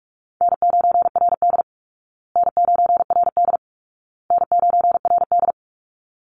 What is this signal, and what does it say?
Morse code, heard as a keyed tone, about 44 wpm, 700 Hz. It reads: N9RD N9RD N9RD
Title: Easy Call Signs Repeated 3x - 45wpm